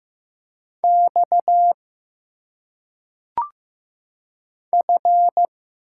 Morse code, heard as a keyed tone, about 15 wpm, 700 Hz. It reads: X F
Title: Single Letters - Phonetic Pronunciation 15wpm